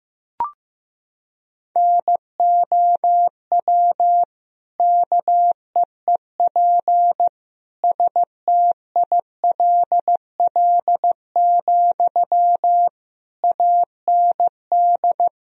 Morse code, heard as a keyed tone, about 16 wpm, 700 Hz. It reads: NOW KEEP STILL, AND
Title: Sentences from Top 1000 Words - Review Part 2 - 15wpm